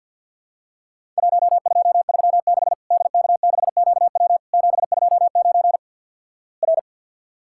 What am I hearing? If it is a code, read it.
Morse code, heard as a keyed tone, about 51 wpm, 700 Hz. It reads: 1236 DK6QW 729 R
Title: Worked All Europe DX Contest - QTC Exchange - 50wpm